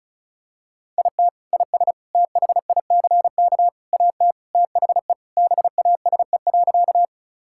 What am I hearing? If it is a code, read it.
Morse code, heard as a keyed tone, about 36 wpm, 700 Hz. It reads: IT IS THICK AT THE BASE.
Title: Sentences from Top 1600 Words 35wpm